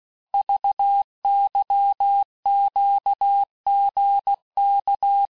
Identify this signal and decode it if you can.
Morse code, heard as a keyed tone, about 16 wpm, 800 Hz. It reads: VYQGK
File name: cours-du-lundi files / Cours-1586-texte-chiffre-vitesse-15mpm